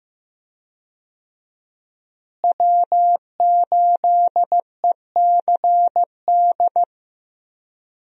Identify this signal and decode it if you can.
Morse code, heard as a keyed tone, about 15 wpm, 700 Hz. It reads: W8ECD
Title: USA Call Signs - 5 Characters - Repeated 3x - 15wpm